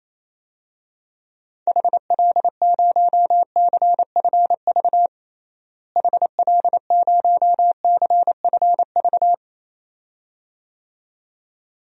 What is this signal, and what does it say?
Morse code, heard as a keyed tone, about 28 wpm, 700 Hz. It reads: HL0CFV HL0CFV
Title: DXCC-100 Call Signs - 6 Characters - Repeated 2x - 28wpm